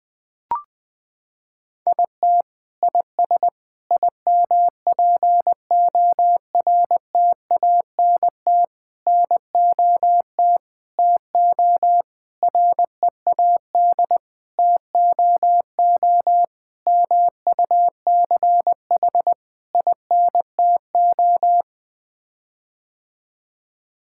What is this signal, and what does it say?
Morse code, heard as a keyed tone, about 20 wpm, 700 Hz. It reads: IT IS IMPORTANT NOT TO READ TOO MUCH INTO
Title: Sets of 9 Words - 20wpm